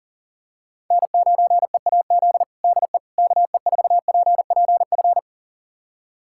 Morse code, heard as a keyed tone, about 40 wpm, 700 Hz. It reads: N9EAZ DE KE4PPF
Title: USA Callsign DE Callsign 40wpm